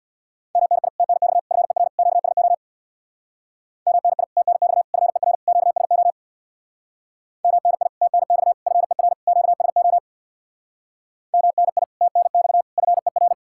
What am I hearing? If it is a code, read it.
Morse code, heard as a keyed tone, about 65 wpm, 700 Hz. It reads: MNI TNX FER QSO MNI TNX FER QSO MNI TNX FER QSO MNI TNX FER